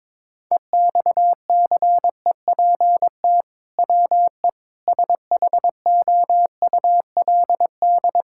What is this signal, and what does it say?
Morse code, heard as a keyed tone, about 22 wpm, 700 Hz. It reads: EXCEPT WE SHOULD
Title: Sentences from Top 1600 Words 22wpm